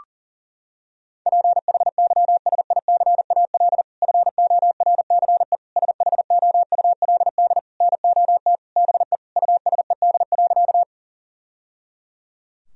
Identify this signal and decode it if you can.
Morse code, heard as a keyed tone, about 40 wpm, 700 Hz. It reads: PHYSICAL FORCE SHOULD NOT BE USED.
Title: Sentences from Top 1000 Words - Review Part 4 - 40wpm